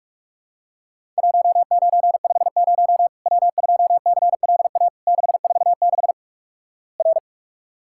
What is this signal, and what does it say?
Morse code, heard as a keyed tone, about 46 wpm, 700 Hz. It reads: 1950 W2CLA 646 R
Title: Worked All Europe DX Contest - QTC Exchange - 45wpm